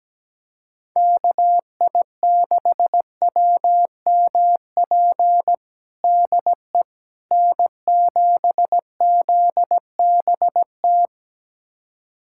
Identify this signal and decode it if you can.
Morse code, heard as a keyed tone, about 17 wpm, 700 Hz. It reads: KI6WMP DE N7ZBT
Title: USA Callsign DE Callsign 17wpm